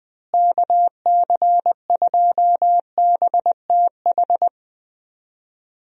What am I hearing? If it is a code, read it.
Morse code, heard as a keyed tone, about 20 wpm, 700 Hz. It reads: KC2BTH
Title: USA Call Signs - 6 Characters - Repeated 2x - 20wpm